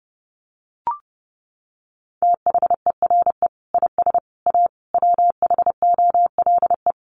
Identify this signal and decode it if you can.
Morse code, heard as a keyed tone, about 30 wpm, 700 Hz. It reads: THERE IS A WHOLE
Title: Sentences from Top 1400 Words 30wpm